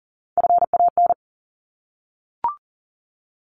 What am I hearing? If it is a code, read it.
Morse code, heard as a keyed tone, about 40 wpm, 700 Hz. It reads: FAN
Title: Top 1001-1250 Words 40wpm